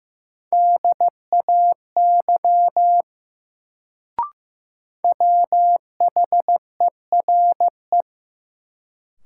Morse code, heard as a keyed tone, about 15 wpm, 700 Hz. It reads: DAY WHERE
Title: Sentences from Top 200 Words - New Word Drill - 15wpm